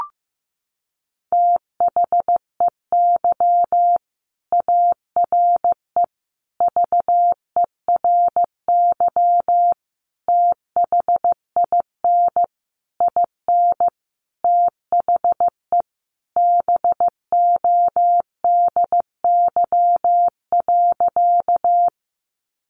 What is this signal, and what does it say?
Morse code, heard as a keyed tone, about 15 wpm, 700 Hz. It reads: THEY ARE VERY THIN IN THE BODY.
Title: Sentences from Top 1200 Words 15wpm